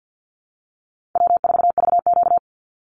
Morse code, heard as a keyed tone, about 50 wpm, 700 Hz. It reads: W4VQ
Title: Call Signs Repeated 2x - 50wpm